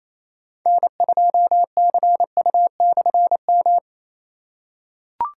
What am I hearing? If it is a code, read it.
Morse code, heard as a keyed tone, about 28 wpm, 700 Hz. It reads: N2CU/M
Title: K1USN SST QSO Practice - 28wpm